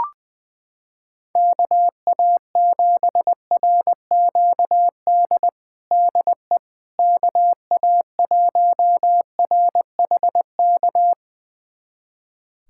Morse code, heard as a keyed tone, about 20 wpm, 700 Hz. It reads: KA7RQD DE KA1RHK
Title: USA Callsign DE Callsign - 6 Characters - 20wpm